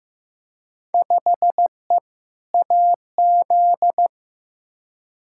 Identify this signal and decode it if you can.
Morse code, heard as a keyed tone, about 15 wpm, 700 Hz. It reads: 5E AZ
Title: Field Day Exchanges - 15wpm